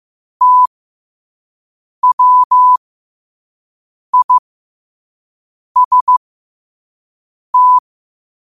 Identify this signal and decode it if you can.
Morse code, heard as a keyed tone, about 15 wpm, spaced as for 5 wpm, 1000 Hz. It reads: TWIST
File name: qotdfiles / 5wpm